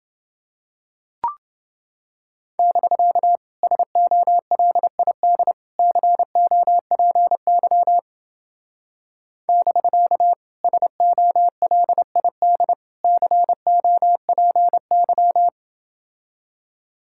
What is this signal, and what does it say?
Morse code, heard as a keyed tone, about 29 wpm, 700 Hz. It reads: <BK> SOLID COPY <BK> SOLID COPY